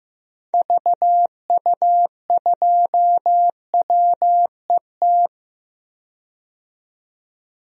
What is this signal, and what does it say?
Morse code, heard as a keyed tone, about 15 wpm, 700 Hz. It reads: VU2WET
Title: DXCC-100 Call Signs - 6 Characters - 15wpm